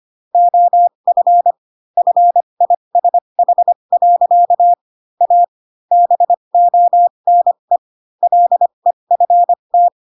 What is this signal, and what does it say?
Morse code, heard as a keyed tone, about 25 wpm, spaced as for 22 wpm, 700 Hz. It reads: OF FISH. A BONE LEFT